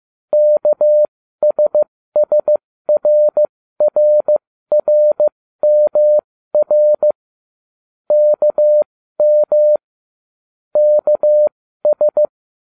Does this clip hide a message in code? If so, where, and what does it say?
Yes: Morse code from 0.3 to 12.3 s, heard as a keyed tone, about 15 wpm, spaced as for 13 wpm, 600 Hz. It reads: KSSRRRMR KM KS